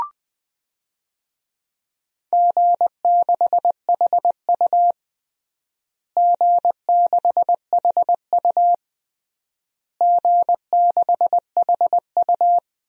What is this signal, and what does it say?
Morse code, heard as a keyed tone, about 20 wpm, 700 Hz. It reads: G6HU G6HU G6HU
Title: DXCC-100 Call Signs - 4 Characters - Repeated 3x - 20wpm